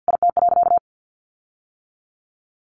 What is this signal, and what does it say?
Morse code, heard as a keyed tone, about 50 wpm, 700 Hz. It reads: IT.
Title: Sentences from Top 1300 Words 50wpm